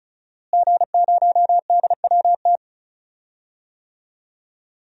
Morse code, heard as a keyed tone, about 36 wpm, 700 Hz. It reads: G0DWT
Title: DXCC-100 Call Signs - 5 Characters - Repeated 2x - 35wpm